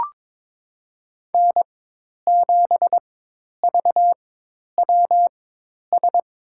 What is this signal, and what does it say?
Morse code, heard as a keyed tone, about 21 wpm, spaced as for 10 wpm, 700 Hz. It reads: N7VWS
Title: USA Call Signs - 5 Characters - 4x Character Spacing - 22wpm